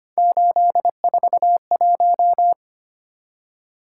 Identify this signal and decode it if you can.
Morse code, heard as a keyed tone, about 25 wpm, 700 Hz. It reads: 841
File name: Three Letter-Number Combinations - 25wpm